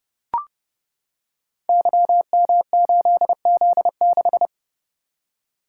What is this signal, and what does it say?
Morse code, heard as a keyed tone, about 28 wpm, 700 Hz. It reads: YM8Z6
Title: Code Groups - 5 Letter:Numbers - 30wpm